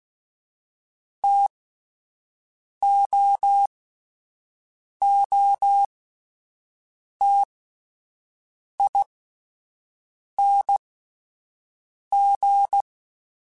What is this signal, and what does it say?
Morse code, heard as a keyed tone, about 16 wpm, spaced as for 5 wpm, 800 Hz. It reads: TOOTING
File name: Difficult Combos / Difficult Combos-mp3s / T AOEI @10